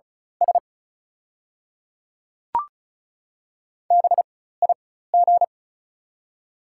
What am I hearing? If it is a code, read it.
Morse code, heard as a keyed tone, about 34 wpm, spaced as for 16 wpm, 690 Hz. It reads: S BIG
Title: 4x Character Spacing - Top 200 Words - 35wpm